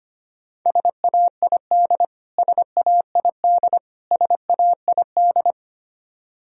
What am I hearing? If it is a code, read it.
Morse code, heard as a keyed tone, about 25 wpm, 700 Hz. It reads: SAID SAID SAID
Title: Top 100 Words Repeated 25wpm